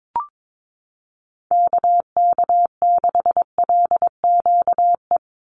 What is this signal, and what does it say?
Morse code, heard as a keyed tone, about 22 wpm, 700 Hz. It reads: KK6LQE
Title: USA Call Signs 22wpm